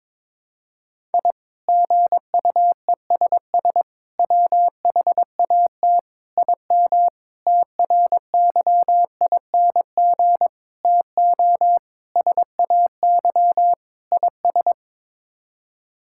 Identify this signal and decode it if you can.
Morse code, heard as a keyed tone, about 22 wpm, 700 Hz. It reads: I GUESS WHAT IM TRYING TO SAY IS
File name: practice / 22wpm / Sets of 8 Words - 22wpm